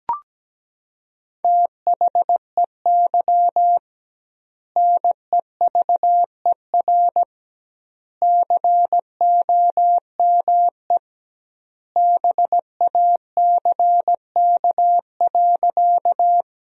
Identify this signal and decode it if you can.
Morse code, heard as a keyed tone, about 17 wpm, 700 Hz. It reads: THEY NEVER COME BACK.
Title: 2x Word-Spacing - Sentences from Top 300 Words - 17wpm